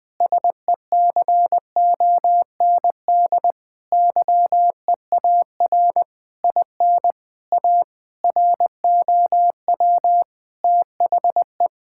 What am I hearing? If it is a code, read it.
Morse code, heard as a keyed tone, about 21 wpm, 700 Hz. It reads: SECOND YEAR IN A ROW THE
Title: Sentences with 8-Word Phrases - Spoken Beginning - 20wpm